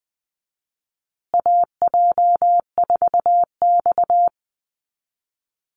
Morse code, heard as a keyed tone, about 20 wpm, 700 Hz. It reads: AJ4X
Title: USA Call Signs Repeated 2x 20wpm